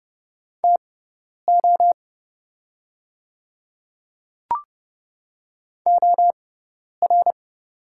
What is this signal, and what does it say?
Morse code, heard as a keyed tone, about 30 wpm, spaced as for 10 wpm, 690 Hz. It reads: TO OR
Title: 6x Character Spacing - Top 200 Words - 30wpm